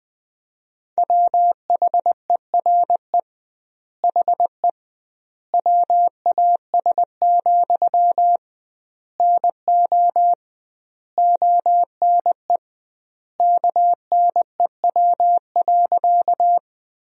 Morse code, heard as a keyed tone, about 21 wpm, 700 Hz. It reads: WHERE HE WAS, NO ONE KNEW.
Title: 2x Word-Spacing - Sentences from Top 400 Words - 20wpm